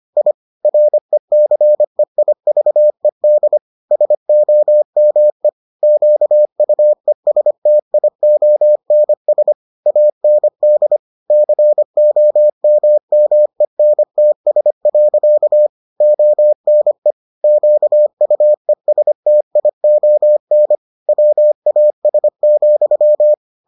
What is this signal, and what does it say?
Morse code, heard as a keyed tone, about 25 wpm, 600 Hz. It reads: I RECEIVED SOME QUESTIONS AND COMMENTS. ONE QUESTION WAS,